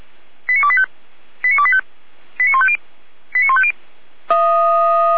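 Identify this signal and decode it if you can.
Five-tone selective call (ZVEI), heard as quick runs of five tones, tones of about 70 ms. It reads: 89285 89285 89150 89150
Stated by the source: ZVEI five-tone selective call received on SDR.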